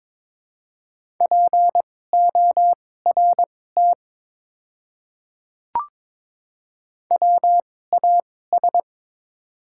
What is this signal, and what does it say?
Morse code, heard as a keyed tone, about 22 wpm, spaced as for 16 wpm, 700 Hz. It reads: PORT WAS
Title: 2x Character Spacing - Top 200 Words - 22wpm